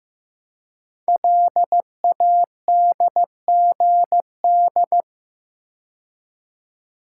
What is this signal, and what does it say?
Morse code, heard as a keyed tone, about 15 wpm, 700 Hz. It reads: LADGD
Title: Code Groups - 5 Letters - 15wpm